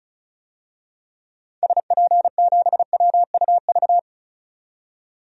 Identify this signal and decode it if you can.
Morse code, heard as a keyed tone, about 36 wpm, 700 Hz. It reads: SP7WUV
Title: DXCC-100 Call Signs - 6 Characters - 35wpm